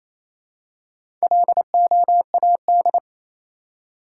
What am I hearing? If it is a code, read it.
Morse code, heard as a keyed tone, about 28 wpm, 700 Hz. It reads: LOAD
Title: Four Letter Words - 28wpm